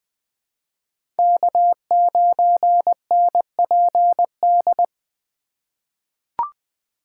Morse code, heard as a keyed tone, about 20 wpm, 700 Hz. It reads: K9NPD
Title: USA Call Signs - 5 Characters - 4x Character Spacing - 20wpm